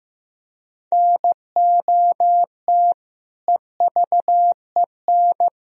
Morse code, heard as a keyed tone, about 16 wpm, 700 Hz. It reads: NOT EVEN
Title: Sentences from Top 1000 Words - Review Part 1 - 15wpm